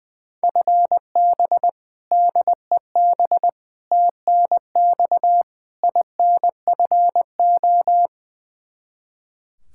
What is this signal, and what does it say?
Morse code, heard as a keyed tone, about 21 wpm, 700 Hz. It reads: FB DEB TNX INFO
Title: K1USN SST QSO Practice 20wpm